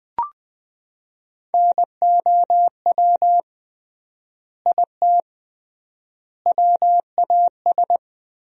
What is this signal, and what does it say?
Morse code, heard as a keyed tone, about 20 wpm, 700 Hz. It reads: NOW IT WAS
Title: 3x Word-Spacing - Sentences from Top 200 Words - 20wpm